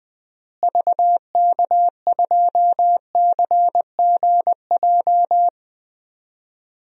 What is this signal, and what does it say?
Morse code, heard as a keyed tone, about 20 wpm, 700 Hz. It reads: VK2CGJ
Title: DXCC-100 Call Signs - 6 Characters - Repeated 3x - 20wpm